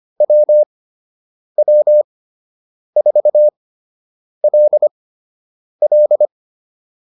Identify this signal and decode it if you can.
Morse code, heard as a keyed tone, about 25 wpm, spaced as for 8 wpm, 600 Hz. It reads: WW4LL